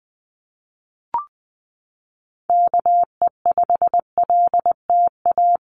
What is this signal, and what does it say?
Morse code, heard as a keyed tone, about 20 wpm, 700 Hz. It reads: KE5LTA
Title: USA Call Signs Repeated 3x 20wpm